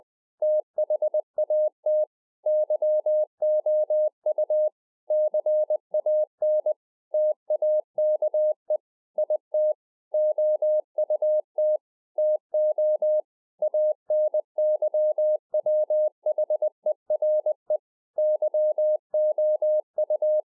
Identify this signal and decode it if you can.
Morse code, heard as a keyed tone, about 20 wpm, 610 Hz. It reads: THAT YOU CAN TAKE IT OUT TO ANYWHERE YOU
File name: 2011-04-mp3 / A SMALL TRANCEIVER 3 20WPM